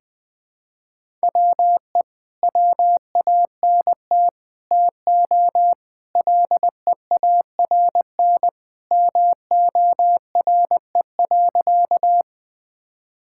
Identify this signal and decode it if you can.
Morse code, heard as a keyed tone, about 20 wpm, 700 Hz. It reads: WE WANT TO LEARN MORE.
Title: Sentences with 3-Word Phrases - 20wpm